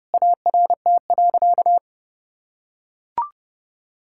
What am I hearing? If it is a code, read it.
Morse code, heard as a keyed tone, about 30 wpm, 700 Hz. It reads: ART.
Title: Sentences from Top 1000 Words - Review Part 6 - 30wpm